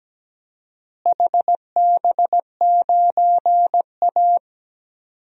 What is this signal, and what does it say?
Morse code, heard as a keyed tone, about 17 wpm, 700 Hz. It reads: HB9A
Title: DXCC-100 Call Signs - 4 Characters - Repeated 2x - 17wpm